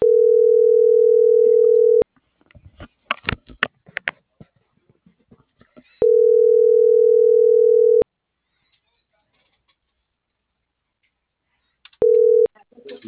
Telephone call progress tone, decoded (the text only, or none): ringback tone